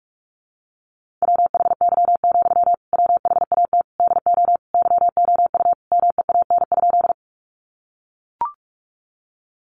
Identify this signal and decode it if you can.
Morse code, heard as a keyed tone, about 44 wpm, 700 Hz. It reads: WHY, WHAT DO YOU MEAN?
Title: Sentences from Top 200 Words 45wpm